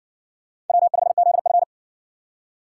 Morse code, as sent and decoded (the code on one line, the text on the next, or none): .-- ..... --.. ..-.
W5ZF